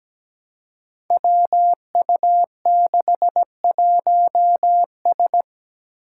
Morse code, heard as a keyed tone, about 17 wpm, 700 Hz. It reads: WU61S
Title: Five Challenging Letter-Number Combinations 17wpm